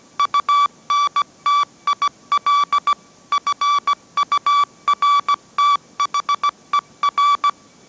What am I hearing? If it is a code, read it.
Morse code, heard as a keyed tone, about 17 wpm, 1150 Hz. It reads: UNTIL FURTHER
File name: mrx / mrx080 18wpm